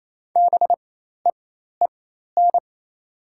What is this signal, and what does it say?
Morse code, heard as a keyed tone, about 28 wpm, spaced as for 13 wpm, 700 Hz. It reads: BEEN
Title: 4x Character Spacing - Top 100 Words - 28wpm